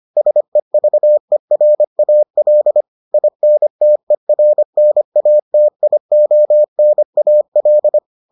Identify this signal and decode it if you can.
Morse code, heard as a keyed tone, about 25 wpm, 600 Hz. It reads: SEVERAL INTERNATIONAL